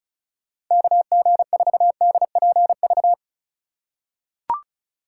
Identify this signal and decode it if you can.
Morse code, heard as a keyed tone, about 36 wpm, 700 Hz. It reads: KG4DPV
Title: USA Call Signs - 6 Characters - 3x Character Spacing - 35wpm